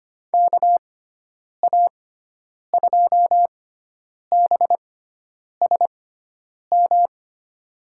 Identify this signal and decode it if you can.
Morse code, heard as a keyed tone, about 25 wpm, spaced as for 9 wpm, 700 Hz. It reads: KA2BSM
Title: USA Call Signs - 6 Characters - 6x Character Spacing - 25wpm